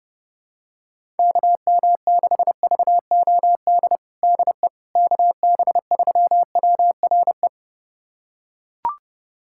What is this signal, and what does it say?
Morse code, heard as a keyed tone, about 30 wpm, 700 Hz. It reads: KM6VOD DE KB3WRE
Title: USA Callsign DE Callsign 30wpm